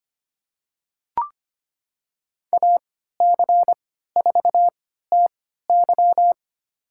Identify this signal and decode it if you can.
Morse code, heard as a keyed tone, about 25 wpm, spaced as for 14 wpm, 700 Hz. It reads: AC4TY
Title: USA Call Signs - 5 Characters - 3x Character Spacing - 25wpm